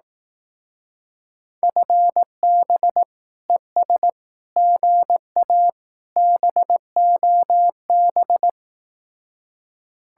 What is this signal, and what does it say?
Morse code, heard as a keyed tone, about 18 wpm, 700 Hz. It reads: FB ES GA BOB
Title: Speed-Racing Common QSO Phrases Encore - 15wpm